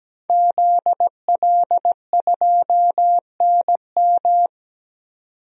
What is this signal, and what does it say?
Morse code, heard as a keyed tone, about 17 wpm, 700 Hz. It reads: ZL2NM
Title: DXCC-100 Call Signs - 5 Characters - 17wpm